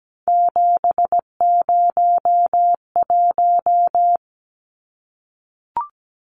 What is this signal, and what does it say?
Morse code, heard as a keyed tone, about 17 wpm, 700 Hz. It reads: 701
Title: Numbers - 3-Digit 17wpm